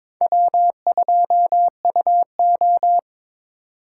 Morse code, heard as a keyed tone, about 22 wpm, 700 Hz. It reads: W2UO
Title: USA Call Signs - 3x Character Spacing - 22wpm